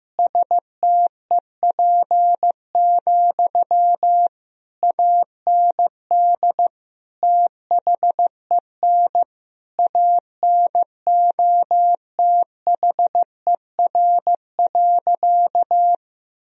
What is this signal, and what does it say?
Morse code, heard as a keyed tone, about 16 wpm, 700 Hz. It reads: STEP, AND THEN ANOTHER.
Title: Sentences from Top 1000 Words - Review Part 2 - 15wpm